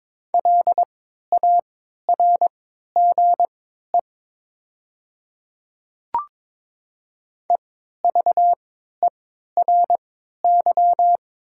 Morse code, heard as a keyed tone, about 22 wpm, spaced as for 13 wpm, 700 Hz. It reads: LARGE EVERY
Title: 3x Character Spacing - Top 500 Words - 22wpm